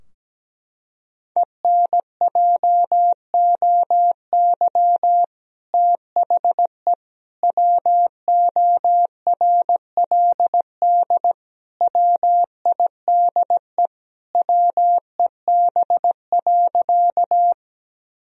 Morse code, heard as a keyed tone, about 17 wpm, 700 Hz. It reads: ENJOY THE WORLD WIDE WEB.
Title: Sentences from Top 1000 Words - Review Part 4 - 17wpm